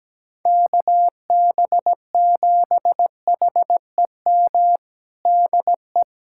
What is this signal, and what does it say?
Morse code, heard as a keyed tone, about 17 wpm, 700 Hz. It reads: KB7HEM DE
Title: USA Callsign DE Callsign - 6 Characters - 17wpm